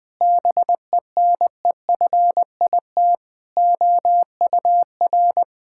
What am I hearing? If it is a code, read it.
Morse code, heard as a keyed tone, about 21 wpm, 700 Hz. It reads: BENEFIT OUR
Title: Sentences from Top 1000 Words - Review Part 6 - 20wpm